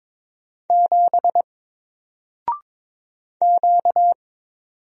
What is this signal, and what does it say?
Morse code, heard as a keyed tone, about 22 wpm, 700 Hz. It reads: 7 Q
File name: Single-Letter-Number-22wpm